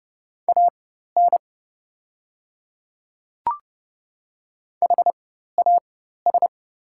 Morse code, heard as a keyed tone, about 30 wpm, spaced as for 14 wpm, 690 Hz. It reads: AN HAS
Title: 4x Character Spacing - Top 100 Words - 30wpm